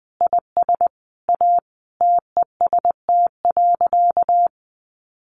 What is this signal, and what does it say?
Morse code, heard as a keyed tone, about 19 wpm, 700 Hz. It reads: IS A TEST.
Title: Additional Easy Phrases 20wpm